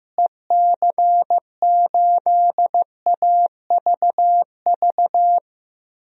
Morse code, heard as a keyed tone, about 16 wpm, 700 Hz. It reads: EC8AVV
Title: DXCC-100 Call Signs - 6 Characters - Repeated 3x - 15wpm